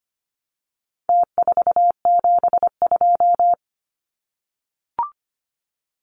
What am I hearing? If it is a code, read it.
Morse code, heard as a keyed tone, about 25 wpm, 700 Hz. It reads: T472
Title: Numbers - Contest Style - 4-Digit 25wpm